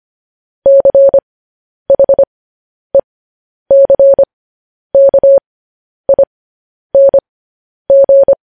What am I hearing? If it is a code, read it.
Morse code, heard as a keyed tone, about 25 wpm, spaced as for 10 wpm, 550 Hz. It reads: CHECKING
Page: news headlines in morse code at 10 wpm